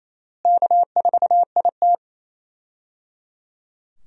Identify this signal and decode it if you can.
Morse code, heard as a keyed tone, about 28 wpm, 700 Hz. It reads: K4IT
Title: USA Call Signs - 4 Characters - Repeated 3x - 28wpm